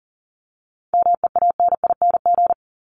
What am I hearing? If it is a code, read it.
Morse code, heard as a keyed tone, about 40 wpm, 700 Hz. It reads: MEANING
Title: ING Suffix - Top 100 Words with Base Words 40wpm